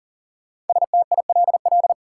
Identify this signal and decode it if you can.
Morse code, heard as a keyed tone, about 40 wpm, 700 Hz. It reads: STILL
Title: Sentences from Top 300 Words - New Word Drill - 40wpm